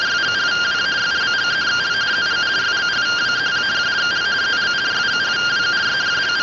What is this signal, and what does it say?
Two-tone teleprinter signal, about 170 Hz shift.